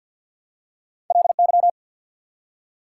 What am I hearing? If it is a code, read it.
Morse code, heard as a keyed tone, about 54 wpm, 690 Hz. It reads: LY
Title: Prefixes and Suffixes 50wpm